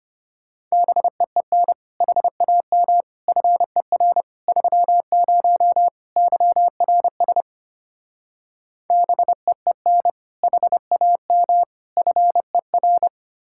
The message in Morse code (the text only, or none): BEEN HAM FER 30 YRS BEEN HAM FER